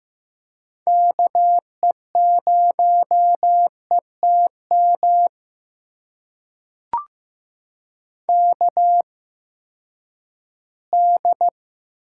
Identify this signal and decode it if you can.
Morse code, heard as a keyed tone, about 15 wpm, spaced as for 10 wpm, 700 Hz. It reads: KE0ETM KD
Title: USA Call Signs - 6 Characters - 8x Character Spacing - 15wpm